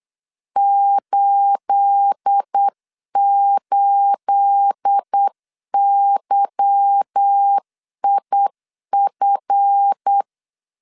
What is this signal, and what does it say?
Morse code, heard as a keyed tone, about 8 wpm, 800 Hz. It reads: 88YIF
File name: LEZIONI / 9 28 8wpm 3512